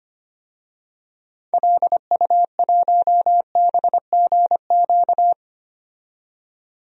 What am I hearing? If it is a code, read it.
Morse code, heard as a keyed tone, about 25 wpm, 700 Hz. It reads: LU1BGQ
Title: DXCC-100 Call Signs - 6 Characters - 25wpm